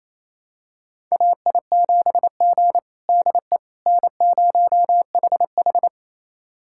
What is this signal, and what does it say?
Morse code, heard as a keyed tone, about 27 wpm, 700 Hz. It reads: AI7G DE N0HH
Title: USA Callsign DE Callsign - 4 Characters - 28wpm